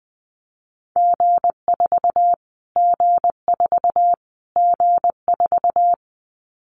Morse code, heard as a keyed tone, about 20 wpm, 700 Hz. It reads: G4 G4 G4
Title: Callsign Prefixes for Top 25 DXCC - Repeated 3x - 20wpm